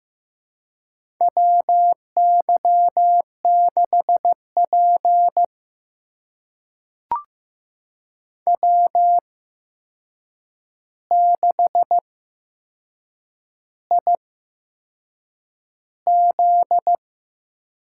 Morse code, heard as a keyed tone, about 15 wpm, spaced as for 6 wpm, 700 Hz. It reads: WY6P W6IZ
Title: USA Call Signs - 4 Characters - 8x Character Spacing - 15wpm